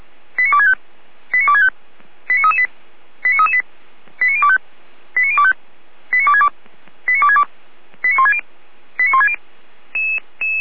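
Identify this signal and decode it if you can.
Five-tone selective call (ZVEI), heard as quick runs of five tones, tones of about 70 ms. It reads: 89265 89265 89338 89338 89025 89025 89262 89262 89160 89160
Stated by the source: ZVEI five-tone selective call received by SDR, fire brigade Orlamünde (feuerwehr-orlamuende).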